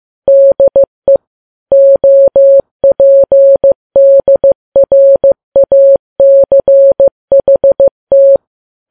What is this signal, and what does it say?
Morse code, heard as a keyed tone, about 15 wpm, 550 Hz. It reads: DE OPDRACHT
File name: Files Web tls / Morse on9fv / morse15kps